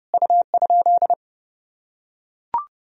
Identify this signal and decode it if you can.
Morse code, heard as a keyed tone, about 32 wpm, 700 Hz. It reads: U?
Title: Sentences from Top 200 Words and Abbreviations 30wpm